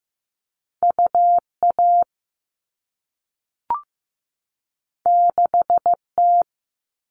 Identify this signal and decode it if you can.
Morse code, heard as a keyed tone, about 15 wpm, 700 Hz. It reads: UA 6T
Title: All Cut Numbers - 2-Digit 15wpm